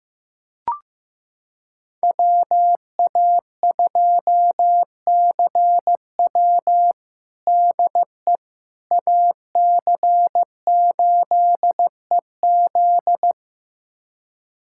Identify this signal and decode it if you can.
Morse code, heard as a keyed tone, about 15 wpm, 700 Hz. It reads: WA2CW DE AC8EZ
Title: USA Callsign DE Callsign - 5 Characters - 15wpm